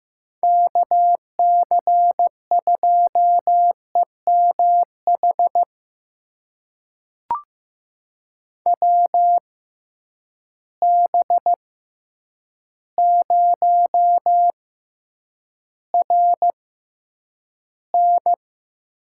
Morse code, heard as a keyed tone, about 15 wpm, spaced as for 8 wpm, 700 Hz. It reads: KC2EMH WB0RN
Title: USA Call Signs - 6 Characters - 6x Character Spacing - 15wpm